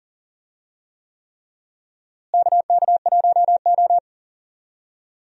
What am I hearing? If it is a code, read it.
Morse code, heard as a keyed tone, about 40 wpm, 700 Hz. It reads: KK1O
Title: USA Call Signs - 4 Characters - Repeated 3x - 40wpm